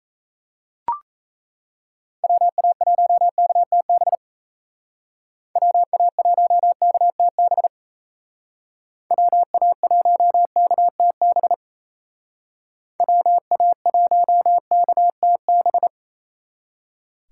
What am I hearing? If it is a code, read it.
Morse code, heard as a keyed tone, about 34 wpm, 700 Hz. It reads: WA1KTB WA1KTB WA1KTB WA1KTB
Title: Speed Racing - USA 2x3 Call Signs 28wpm